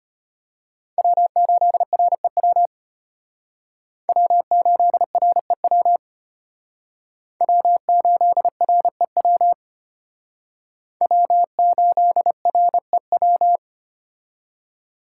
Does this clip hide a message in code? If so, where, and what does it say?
Yes: Morse code from 1.0 to 13.6 s, heard as a keyed tone, about 32 wpm, 700 Hz. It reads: W8REW W8REW W8REW W8REW